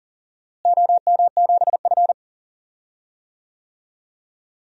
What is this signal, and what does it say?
Morse code, heard as a keyed tone, about 37 wpm, 700 Hz. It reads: OM7F
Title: DXCC-100 Call Signs - 4 Characters - Repeated 2x - 40wpm